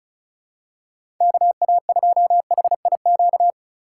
Morse code, heard as a keyed tone, about 34 wpm, 700 Hz. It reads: KA2HIQ
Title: USA Call Signs - 4x Character Spacing - 35wpm